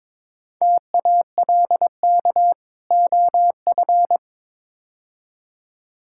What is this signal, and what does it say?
Morse code, heard as a keyed tone, about 22 wpm, 700 Hz. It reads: TALK OF
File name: practice / 22wpm / Build A Phrase - 5-Word Phrases - 22wpm